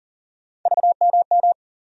Morse code, heard as a keyed tone, about 40 wpm, 700 Hz. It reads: VMM
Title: USA Call Sign Postfixes - 3 Letters 40wpm